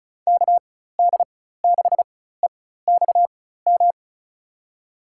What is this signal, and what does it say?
Morse code, heard as a keyed tone, about 34 wpm, spaced as for 16 wpm, 700 Hz. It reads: KD6EXM
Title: USA Call Signs - 6 Characters - 4x Character Spacing - 35wpm